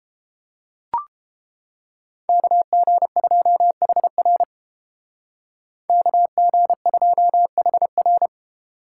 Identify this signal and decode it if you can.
Morse code, heard as a keyed tone, about 30 wpm, 700 Hz. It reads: KG2HR KG2HR